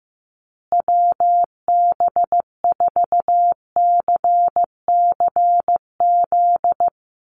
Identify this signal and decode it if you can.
Morse code, heard as a keyed tone, about 15 wpm, 700 Hz. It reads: WB4CCZ
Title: USA Call Signs Repeated 2x 15wpm